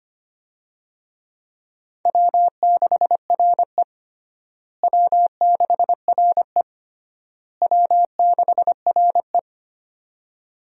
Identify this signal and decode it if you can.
Morse code, heard as a keyed tone, about 25 wpm, 700 Hz. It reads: W6RE W6RE W6RE
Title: USA Call Signs - 4 Characters - Repeated 3x - 25wpm